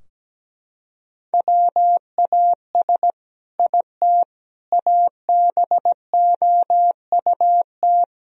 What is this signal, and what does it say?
Morse code, heard as a keyed tone, about 17 wpm, 700 Hz. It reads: WAS IT ABOUT
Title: Sentences from Top 1000 Words - Review Part 3 - 17wpm